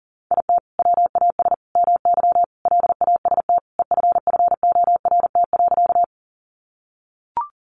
Morse code, heard as a keyed tone, about 40 wpm, 700 Hz. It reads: IT WAS MY LAST EFFORT.
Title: Sentences from Top 1200 Words 40wpm